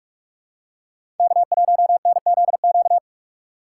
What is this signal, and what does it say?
Morse code, heard as a keyed tone, about 46 wpm, 700 Hz. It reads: K1NZQ
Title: USA Call Signs - 5 Characters - 2x Character Spacing - 45wpm